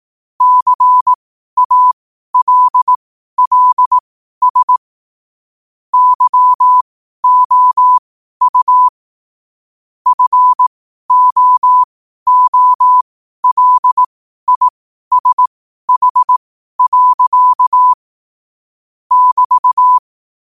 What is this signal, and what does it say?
Morse code, heard as a keyed tone, about 18 wpm, spaced as for 12 wpm, 1000 Hz. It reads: CALLS YOU FOOLISH. <BT>
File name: qotdfiles / 12wpm